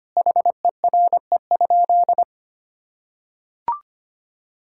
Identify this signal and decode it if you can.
Morse code, heard as a keyed tone, about 25 wpm, 700 Hz. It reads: HERE?
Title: Sentences from Top 1000 Words - Review Part 5 - 25wpm